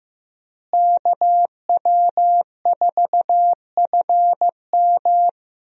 Morse code, heard as a keyed tone, about 15 wpm, 700 Hz. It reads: KW4FM
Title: USA Call Signs - 5 Characters - Repeated 3x - 15wpm